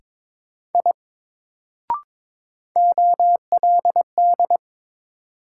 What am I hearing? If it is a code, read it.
Morse code, heard as a keyed tone, about 22 wpm, 700 Hz. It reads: I OLD
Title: Sentences from Top 200 Words - New Word Drill - 22wpm